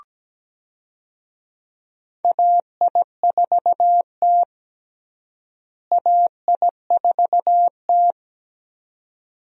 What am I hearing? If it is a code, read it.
Morse code, heard as a keyed tone, about 17 wpm, 700 Hz. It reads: AI4T AI4T
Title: USA Call Signs - 4 Characters - Repeated 2x - 17wpm